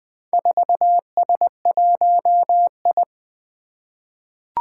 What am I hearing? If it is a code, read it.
Morse code, heard as a keyed tone, about 20 wpm, 700 Hz. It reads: 4S1I
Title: Four Challenging Letter-Number Combinations 20wpm